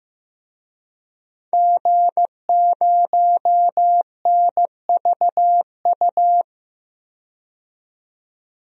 Morse code, heard as a keyed tone, about 15 wpm, 700 Hz. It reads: G0NVU
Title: DXCC-100 Call Signs - 5 Characters - 15wpm